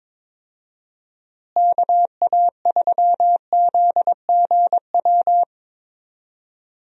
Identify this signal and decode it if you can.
Morse code, heard as a keyed tone, about 22 wpm, 700 Hz. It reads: KA3ZGW
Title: USA Call Signs - 6 Characters - Repeated 3x - 22wpm